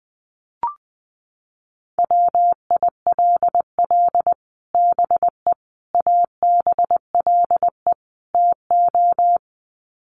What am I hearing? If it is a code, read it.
Morse code, heard as a keyed tone, about 20 wpm, 700 Hz. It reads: WILL BE ABLE TO
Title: Sets of 4 Words 20wpm